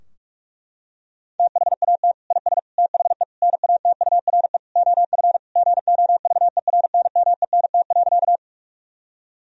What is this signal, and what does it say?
Morse code, heard as a keyed tone, about 46 wpm, 700 Hz. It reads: THAT IS THE NATURE OF GOVERNMENT.
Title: Sentences from Top 1000 Words - Review Part 3 - 45wpm